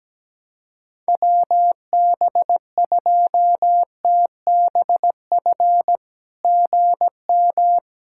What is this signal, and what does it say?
Morse code, heard as a keyed tone, about 17 wpm, 700 Hz. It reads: WB2TBF GM
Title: SOTA QSO Practice 17wpm